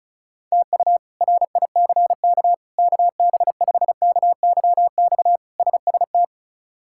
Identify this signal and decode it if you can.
Morse code, heard as a keyed tone, about 34 wpm, 700 Hz. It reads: TU RICK KB5KYX SST
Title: K1USN SST QSO Practice - 35wpm